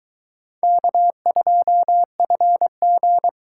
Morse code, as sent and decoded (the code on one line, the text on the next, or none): -.- ..--- ..-. --.
K2FG